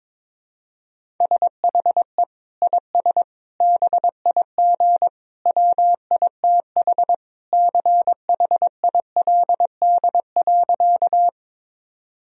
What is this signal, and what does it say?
Morse code, heard as a keyed tone, about 22 wpm, 700 Hz. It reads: SHE IS BIG WITH CHILD.
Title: Sentences from Top 1000 Words - Review Part 1 - 22wpm